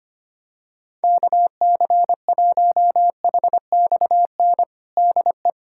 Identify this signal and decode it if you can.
Morse code, heard as a keyed tone, about 25 wpm, 700 Hz. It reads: KC1HXN DE
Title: USA Callsign DE Callsign - 6 Characters - 25wpm